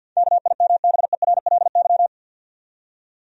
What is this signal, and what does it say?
Morse code, heard as a keyed tone, about 53 wpm, 700 Hz. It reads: KIMBERLY
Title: USA Common Names - 50wpm